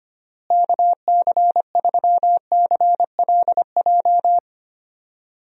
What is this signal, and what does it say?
Morse code, heard as a keyed tone, about 25 wpm, 700 Hz. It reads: KC3CLJ
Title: USA Call Signs - 6 Characters - Repeated 3x - 25wpm